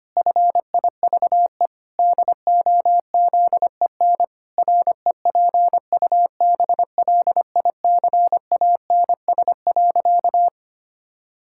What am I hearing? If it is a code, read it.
Morse code, heard as a keyed tone, about 25 wpm, 700 Hz. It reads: FIVE DOZEN REPUBLICANS.
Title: Pangram Challenge v2 - 25wpm